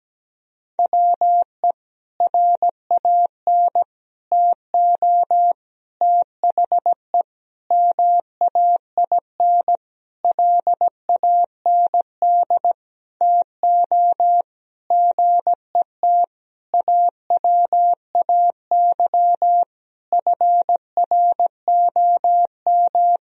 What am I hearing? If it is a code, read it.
Morse code, heard as a keyed tone, about 17 wpm, 700 Hz. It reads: WE RAN TO THE MAINLAND TO GET AWAY FROM
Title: Sentences from Top 1000 Words - Review Part 2 - 17wpm